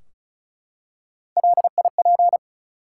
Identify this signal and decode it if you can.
Morse code, heard as a keyed tone, about 36 wpm, 700 Hz. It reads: LIP
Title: Three Letter Words - 35wpm